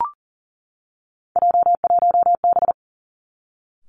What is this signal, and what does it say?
Morse code, heard as a keyed tone, about 40 wpm, 700 Hz. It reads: J1B
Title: Three Challenging Letter-Number Combinations 40wpm